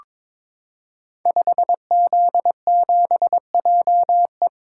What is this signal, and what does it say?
Morse code, heard as a keyed tone, about 22 wpm, 700 Hz. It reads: 5Z7JE
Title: Code Groups - 5 Letter:Numbers - 22wpm